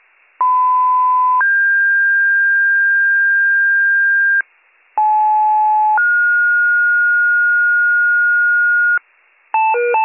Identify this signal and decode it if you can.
Two-tone page, heard as a one-second tone, then a three-second tone, about 1010 then 1600 Hz, about 860 then 1430 Hz, with noise. It is heard from 0.4 to 9.0 s.